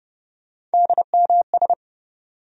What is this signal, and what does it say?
Morse code, heard as a keyed tone, about 32 wpm, 700 Hz. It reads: DMS
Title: USA Call Sign Postfixes - 3 Letters 30wpm